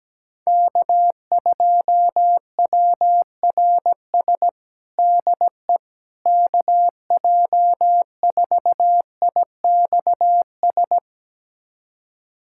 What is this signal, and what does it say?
Morse code, heard as a keyed tone, about 17 wpm, 700 Hz. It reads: K2WRS DE KJ4IXS
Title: USA Callsign DE Callsign 17wpm